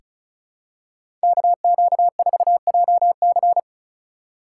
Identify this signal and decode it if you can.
Morse code, heard as a keyed tone, about 36 wpm, 700 Hz. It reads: KQ4JC
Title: USA Call Signs - 5 Characters - 4x Character Spacing - 35wpm